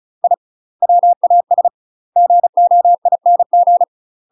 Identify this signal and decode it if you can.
Morse code, heard as a keyed tone, about 36 wpm, 700 Hz. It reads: I WAS GOING
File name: UK MASTER QSOs TRAINING / 35WPM / UK MASTER QSOs 0028